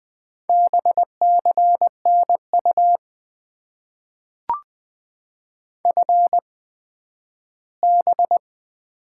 Morse code, heard as a keyed tone, about 20 wpm, spaced as for 12 wpm, 700 Hz. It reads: BCNU FB
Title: Top 90 QSO Elements - 8x Character Spacing - 20wpm